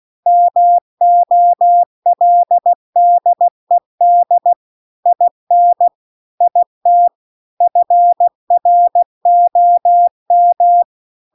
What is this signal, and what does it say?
Morse code, heard as a keyed tone, about 16 wpm, 700 Hz. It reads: MOLDED IN IT FROM